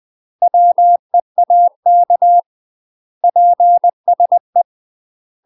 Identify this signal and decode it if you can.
Morse code, heard as a keyed tone, about 20 wpm, 700 Hz. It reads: WEAK PSE
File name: UK MASTER QSOs TRAINING / 20WPM / UK MASTER QSOs 0003